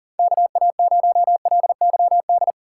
Morse code, heard as a keyed tone, about 40 wpm, 700 Hz. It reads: KA0LYD
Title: USA Call Signs - 6 Characters - 4x Character Spacing - 40wpm